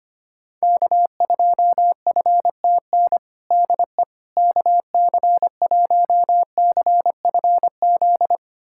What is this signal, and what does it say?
Morse code, heard as a keyed tone, about 25 wpm, 700 Hz. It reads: K2FTN DE KC1CFZ
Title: USA Callsign DE Callsign 25wpm